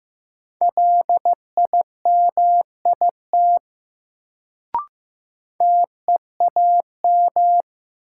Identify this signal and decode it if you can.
Morse code, heard as a keyed tone, about 15 wpm, 700 Hz. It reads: LIMIT TEAM
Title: Sentences from Top 1500 Words - New Word Drill - 15wpm